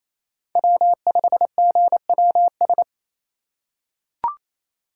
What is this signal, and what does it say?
Morse code, heard as a keyed tone, about 28 wpm, 700 Hz. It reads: W5GWS
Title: USA Call Signs - 5 Characters - 4x Character Spacing - 28wpm